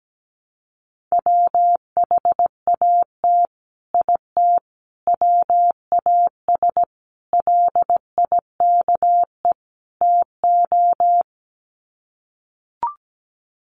Morse code, heard as a keyed tone, about 17 wpm, 700 Hz. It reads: WHAT IT WAS LIKE TO
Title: Sets of 5 Words Encore 17wpm